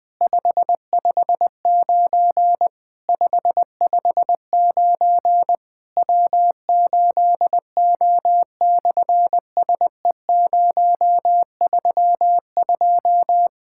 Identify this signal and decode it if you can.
Morse code, heard as a keyed tone, about 19 wpm, 700 Hz. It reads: 559 559 W8O/SE032
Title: SOTA QSO Practice 20wpm